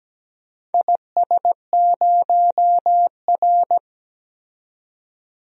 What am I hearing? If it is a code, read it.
Morse code, heard as a keyed tone, about 17 wpm, 700 Hz. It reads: IS0R
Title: DXCC-100 Call Signs - 4 Characters - 17wpm